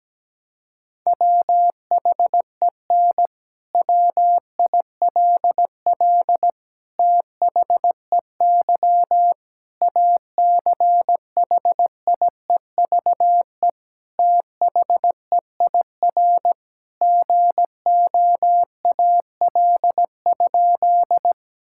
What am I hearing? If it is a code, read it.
Morse code, heard as a keyed tone, about 17 wpm, 700 Hz. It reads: WHEN WILL THEY ACHIEVE THEIR GOAL?
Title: Sentences from Top 1000 Words - Review Part 6 - 17wpm